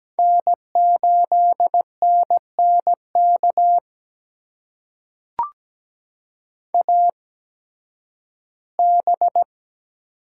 Morse code, heard as a keyed tone, about 17 wpm, spaced as for 11 wpm, 700 Hz. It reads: N8NNK AB
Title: USA Call Signs - 8x Character Spacing - 17wpm